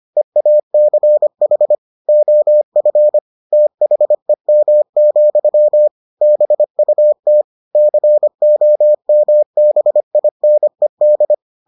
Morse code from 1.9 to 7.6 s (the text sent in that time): OF THEM, BUT